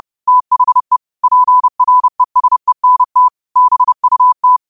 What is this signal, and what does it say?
Morse code, heard as a keyed tone, about 30 wpm, 1000 Hz. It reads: THE PRESENT BUT
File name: qotdfiles / 30wpm